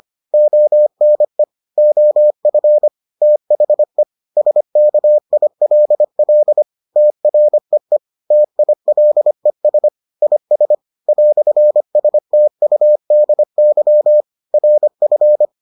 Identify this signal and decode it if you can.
Morse code, heard as a keyed tone, about 25 wpm, 600 Hz. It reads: ONE OF THE SKILL TREE TILES IS "STUDY RF